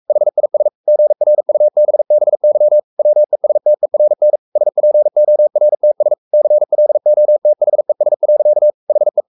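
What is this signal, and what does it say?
Morse code, heard as a keyed tone, about 44 wpm, 600 Hz. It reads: HIS GRUBBY WESTERN SPORTS CLOTHES. HE
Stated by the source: Morse code at 43 wpm sending Ken Follett's novel Never in English, from CW med Gnister.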